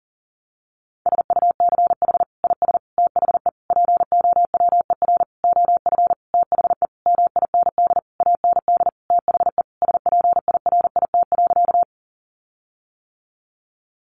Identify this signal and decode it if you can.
Morse code, heard as a keyed tone, about 40 wpm, 700 Hz. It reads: SUCH IS THE POWER OF THE MIND AND THE SPIRIT.
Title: Sentences from Top 1400 Words 40wpm